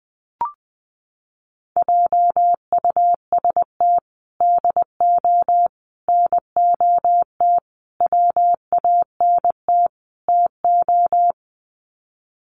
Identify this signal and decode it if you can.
Morse code, heard as a keyed tone, about 20 wpm, 700 Hz. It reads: JUST DO NOT WANT TO
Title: Sets of 5 Words Encore 20wpm